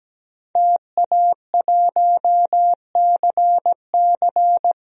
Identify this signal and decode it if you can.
Morse code, heard as a keyed tone, about 17 wpm, 700 Hz. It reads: TA1CC
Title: DXCC-100 Call Signs - 5 Characters - 17wpm